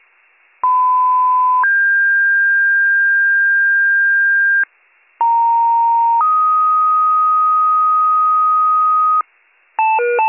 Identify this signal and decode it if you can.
Two-tone page, heard as a one-second tone, then a three-second tone, about 1010 then 1600 Hz, about 930 then 1250 Hz, with noise. It is heard from 0.6 to 9.2 s.